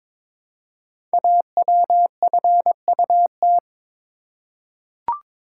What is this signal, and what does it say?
Morse code, heard as a keyed tone, about 22 wpm, 700 Hz. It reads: AWFUT
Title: Code Groups - 5 Letters - 22wpm